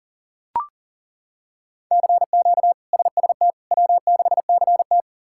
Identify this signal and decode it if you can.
Morse code, heard as a keyed tone, about 40 wpm, 700 Hz. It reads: CQ SST W6CT
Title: K1USN SST QSO Practice - 40wpm